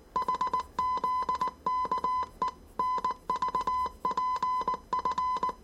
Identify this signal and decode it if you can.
Morse code, heard as a keyed tone, about 19 wpm, 1010 Hz. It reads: HZKE NVPF
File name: 业余无线电 / ●未分类 / 字码80